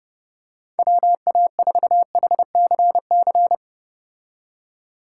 Morse code, heard as a keyed tone, about 30 wpm, 700 Hz. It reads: WA4HCC
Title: K1USN SST QSO Practice - 30wpm